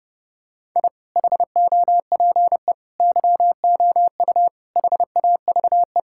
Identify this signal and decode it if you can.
Morse code, heard as a keyed tone, about 32 wpm, 700 Hz. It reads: I HOPE YOU HAVE
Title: Sentences from Top 1000 Words - Review Part 6 - 30wpm